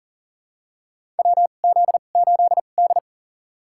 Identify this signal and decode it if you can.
Morse code, heard as a keyed tone, about 40 wpm, spaced as for 29 wpm, 700 Hz. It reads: WZ8D
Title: USA Call Signs - 4 Characters - 2x Character Spacing - 40wpm